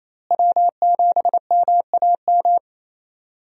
Morse code, heard as a keyed tone, about 28 wpm, 700 Hz. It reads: W7MAM
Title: USA Call Signs - 5 Characters - Repeated 3x - 28wpm